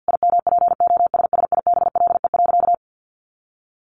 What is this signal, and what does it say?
Morse code, heard as a keyed tone, about 50 wpm, 700 Hz. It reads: IMPOSSIBLE.
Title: Sentences from Top 1300 Words 50wpm